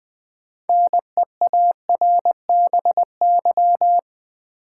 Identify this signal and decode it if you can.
Morse code, heard as a keyed tone, about 20 wpm, 700 Hz. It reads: NEARBY
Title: Build A Word - 3 Letters At A Time - 6-Letter Words - 20wpm